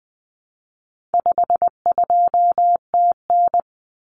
Morse code, heard as a keyed tone, about 20 wpm, 700 Hz. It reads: 52TN
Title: Common Cut Numbers - 4-Digit 20wpm